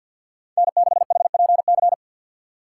Morse code, heard as a keyed tone, about 52 wpm, 700 Hz. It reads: N6HPC
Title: USA Call Signs - 5 Characters - 2x Character Spacing - 50wpm